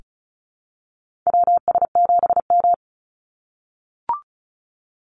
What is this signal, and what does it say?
Morse code, heard as a keyed tone, about 34 wpm, 700 Hz. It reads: WS7M
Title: Call Signs - 35wpm